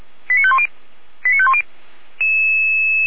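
Five-tone selective call (ZVEI), heard as quick runs of five tones, tones of about 80 ms. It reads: 89510 89510
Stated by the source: ZVEI five-tone selective call received by SDR, from the Feuerwehr Orlamünde fire brigade audio server.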